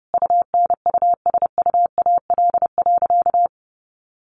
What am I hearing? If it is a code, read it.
Morse code, heard as a keyed tone, about 30 wpm, 700 Hz. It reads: UNUSUAL.
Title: Sentences from Top 1400 Words 30wpm